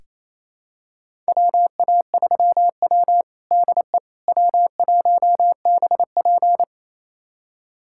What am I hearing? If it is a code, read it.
Morse code, heard as a keyed tone, about 28 wpm, 700 Hz. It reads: WA3W DE W1BP
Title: USA Callsign DE Callsign - 4 Characters - 28wpm